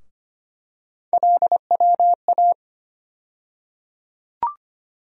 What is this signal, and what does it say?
Morse code, heard as a keyed tone, about 25 wpm, 700 Hz. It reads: LWA
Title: USA Call Sign Postfixes - 3 Letters 25wpm